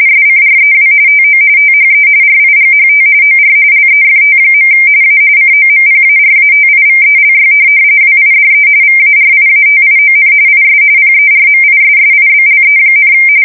RTTY, about 45.45 baud. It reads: SCREW TO THE RIGHT OR LEFT. A COMPARATIVELY WIDE VARIATION IS PERMISSIBLE,
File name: fsk170 / morkrum